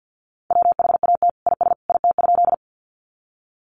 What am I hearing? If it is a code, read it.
Morse code, heard as a keyed tone, about 48 wpm, 700 Hz. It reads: WHAT IS IT?
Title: Sentences from Top 100 Words 50wpm